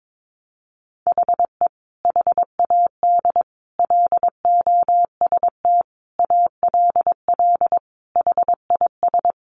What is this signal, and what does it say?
Morse code, heard as a keyed tone, about 22 wpm, 700 Hz. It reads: HE HAD LOST ALL HIS
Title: Sentences from Top 1400 Words 22wpm